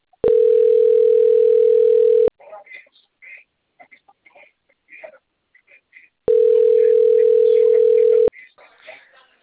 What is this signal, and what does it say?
Ringback tone (North America), 440 + 480 Hz, with speech.
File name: rg-1600-unknown- - - .12041